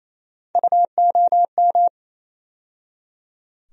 Morse code, heard as a keyed tone, about 27 wpm, 700 Hz. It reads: UOM
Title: USA Call Sign Postfixes - 3 Letters 28wpm